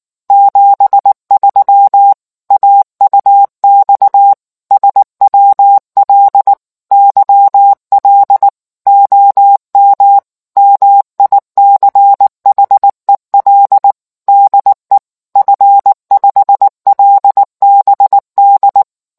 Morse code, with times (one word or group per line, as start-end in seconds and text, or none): 0.3-2.1 73
2.5-4.3 AUX
4.7-6.5 SWL
6.9-8.5 YL
8.9-10.2 OM
10.6-13.9 MICHEL
14.3-15.0 DE
15.4-18.8 F5LBD